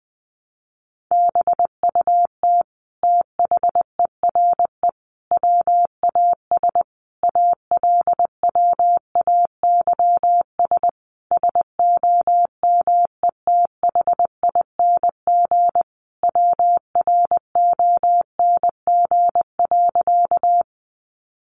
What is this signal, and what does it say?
Morse code, heard as a keyed tone, about 20 wpm, 700 Hz. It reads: BUT THERE WAS ALWAYS SOMETHING WRONG.
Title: Sentences from Top 900 Words 20wpm